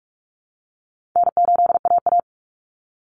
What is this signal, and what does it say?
Morse code, heard as a keyed tone, about 44 wpm, 700 Hz. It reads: N8AA
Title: Call Signs Repeated 3x - 45wpm